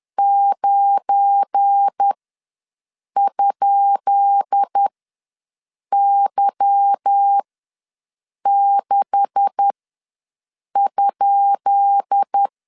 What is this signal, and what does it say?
Morse code, heard as a keyed tone, about 10 wpm, spaced as for 6 wpm, 800 Hz. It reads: 9?Y6?
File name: LEZIONI / 10 SIGNS / 10B INT ws 10wpm 4134